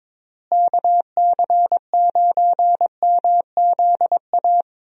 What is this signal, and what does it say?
Morse code, heard as a keyed tone, about 21 wpm, 700 Hz. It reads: KC9MZA
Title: USA Call Signs - 6 Characters - Repeated 3x - 22wpm